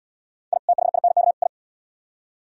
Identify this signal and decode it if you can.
Morse code, heard as a keyed tone, about 71 wpm, 700 Hz. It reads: I THINK I